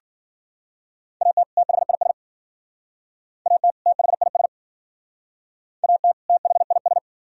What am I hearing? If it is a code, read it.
Morse code, heard as a keyed tone, about 51 wpm, 700 Hz. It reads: AT THIS AT THIS AT THIS